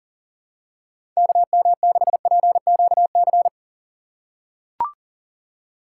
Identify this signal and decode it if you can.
Morse code, heard as a keyed tone, about 40 wpm, 700 Hz. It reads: KM6PQC
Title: USA Call Signs - 6 Characters - 6x Character Spacing - 40wpm